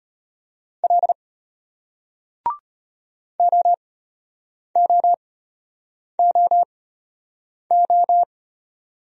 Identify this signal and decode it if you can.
Morse code, heard as a keyed tone, about 34 wpm, spaced as for 8 wpm, 700 Hz. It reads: L OOOO